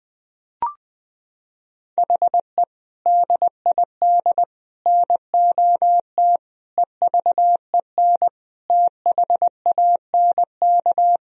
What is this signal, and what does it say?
Morse code, heard as a keyed tone, about 20 wpm, 700 Hz. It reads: HE DID NOT EVEN THANK
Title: Sentences from Top 1000 Words - Review Part 1 - 20wpm